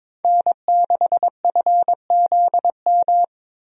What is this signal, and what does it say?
Morse code, heard as a keyed tone, about 22 wpm, 700 Hz. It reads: N6FZM
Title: USA Call Signs - 5 Characters - Repeated 3x - 22wpm